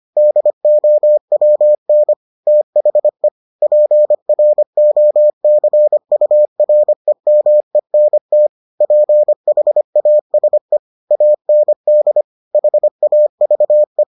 Morse code, heard as a keyed tone, about 25 wpm, 600 Hz. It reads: DOWN THE PROCUREMENT PHASE AND HAVE